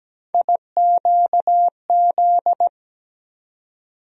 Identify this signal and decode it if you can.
Morse code, heard as a keyed tone, about 17 wpm, 700 Hz. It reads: IQZ
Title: USA Call Sign Postfixes - 3 Letters 17wpm